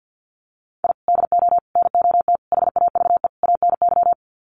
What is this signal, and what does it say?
Morse code, heard as a keyed tone, about 50 wpm, 700 Hz. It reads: I DO NOT HAVE ANY
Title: Sets of 5 Words 50wpm